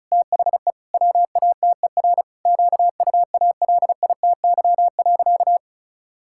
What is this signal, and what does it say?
Morse code, heard as a keyed tone, about 34 wpm, 700 Hz. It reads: THE WATER QUALITY.
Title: Sentences from Top 1000 Words - Review Part 5 - 35wpm